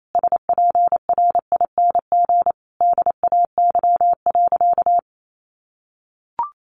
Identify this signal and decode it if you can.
Morse code, heard as a keyed tone, about 28 wpm, 700 Hz. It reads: SPRING DAY.
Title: Sentences from Top 1200 Words 28wpm